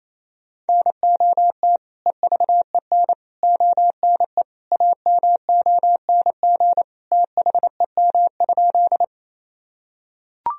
Morse code, heard as a keyed tone, about 28 wpm, 700 Hz. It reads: NOT EVEN ONE AMONG THEM?
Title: Sentences from Top 1000 Words - Review Part 3 - 28wpm